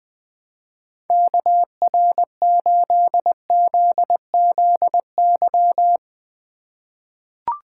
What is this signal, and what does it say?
Morse code, heard as a keyed tone, about 20 wpm, 700 Hz. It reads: KR8ZZY
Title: USA Call Signs - 6 Characters - 2x Character Spacing - 20wpm